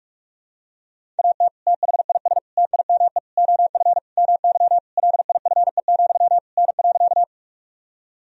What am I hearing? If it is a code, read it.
Morse code, heard as a keyed tone, about 46 wpm, 700 Hz. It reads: AT THIS TIME OF MY LIFE, N.
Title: Sentences from Top 300 Words and Abbreviations 45wpm